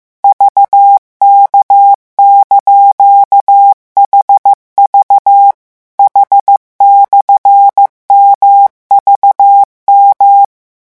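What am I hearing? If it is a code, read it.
Morse code, heard as a keyed tone, about 15 wpm, 800 Hz. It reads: VK)HV H/MVM
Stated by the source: Morse revision lesson at 14 words per minute.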